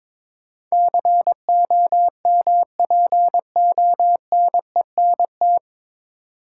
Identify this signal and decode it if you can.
Morse code, heard as a keyed tone, about 22 wpm, 700 Hz. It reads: COMPONENT
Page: Morse Code Ninja practice 22wpm